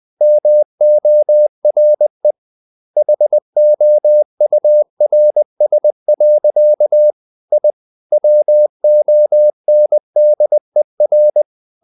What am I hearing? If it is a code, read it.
Morse code, heard as a keyed tone, about 20 wpm, 600 Hz. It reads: MORE HOURS. I WONDER
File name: the old man and the sea / the old man and the sea part 1 20wpm